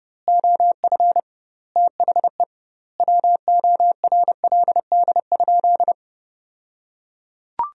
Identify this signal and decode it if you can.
Morse code, heard as a keyed tone, about 30 wpm, 700 Hz. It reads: OF THE WORLD?
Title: 2x Word-Spacing - Sentences from Top 200 Words - 30wpm